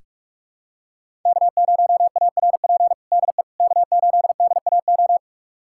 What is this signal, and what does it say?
Morse code, heard as a keyed tone, about 46 wpm, 700 Hz. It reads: K0ARP DE K8DAO
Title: USA Callsign DE Callsign - 5 Characters - 45wpm